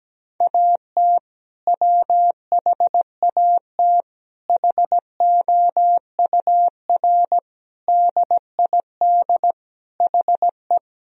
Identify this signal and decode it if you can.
Morse code, heard as a keyed tone, about 17 wpm, 700 Hz. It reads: AT WHAT HOUR DID HE
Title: Sentences from Top 1000 Words - Review Part 4 - 17wpm